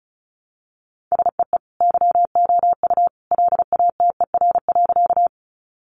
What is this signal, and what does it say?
Morse code, heard as a keyed tone, about 34 wpm, 700 Hz. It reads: SEE YOU LATER.
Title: Sentences from Top 700 Words 35wpm